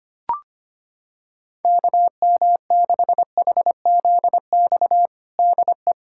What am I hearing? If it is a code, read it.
Morse code, heard as a keyed tone, about 25 wpm, 700 Hz. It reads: KM6HZX DE
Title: USA Callsign DE Callsign 25wpm